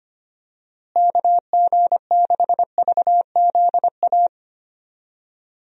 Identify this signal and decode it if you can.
Morse code, heard as a keyed tone, about 25 wpm, 700 Hz. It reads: KG6VZA
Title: USA Call Signs - 6 Characters - Repeated 3x - 25wpm